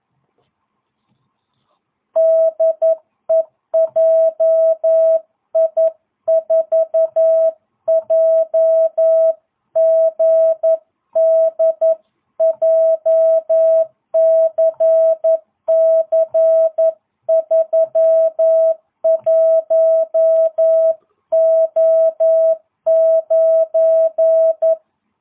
Morse code, heard as a keyed tone, about 11 wpm, 650 Hz. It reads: DE JI4JGD JCC31O9